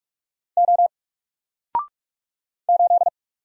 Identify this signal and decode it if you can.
Morse code, heard as a keyed tone, about 44 wpm, 700 Hz. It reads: O 8
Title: Single Character (Letter:Number:Punctuation:Prosign) 45wpm